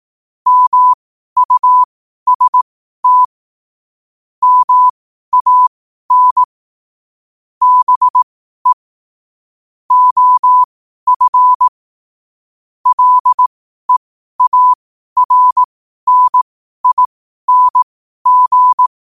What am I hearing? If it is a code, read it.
Morse code, heard as a keyed tone, about 18 wpm, spaced as for 12 wpm, 1000 Hz. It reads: MUST MAN BE OF LEARNING
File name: qotdfiles / 12wpm